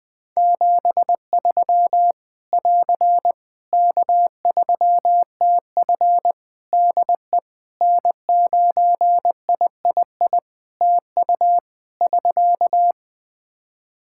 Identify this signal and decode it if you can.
Morse code, heard as a keyed tone, about 21 wpm, 700 Hz. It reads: 73 <AR> K3TF DE N9III TU <SK>
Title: Practice QSO - Exchange 4 - Ending - 20wpm